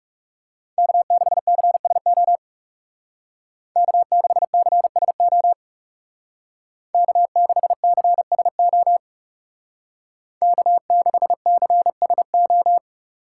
Morse code, heard as a keyed tone, about 37 wpm, 700 Hz. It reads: K6CSO K6CSO K6CSO K6CSO
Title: Speed Racing - USA 1x3 Call Signs 30wpm